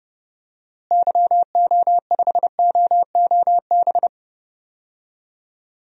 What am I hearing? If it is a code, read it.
Morse code, heard as a keyed tone, about 30 wpm, 700 Hz. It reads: YO5OOB
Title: DXCC-100 Call Signs - 6 Characters - Repeated 3x - 30wpm